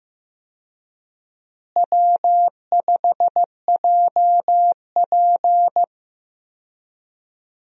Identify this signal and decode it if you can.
Morse code, heard as a keyed tone, about 15 wpm, 700 Hz. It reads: W5JP
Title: USA Call Signs - 4 Characters - Repeated 3x - 15wpm